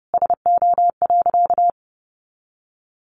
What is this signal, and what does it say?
Morse code, heard as a keyed tone, about 28 wpm, 700 Hz. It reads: SO.
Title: Sentences from Top 1400 Words 30wpm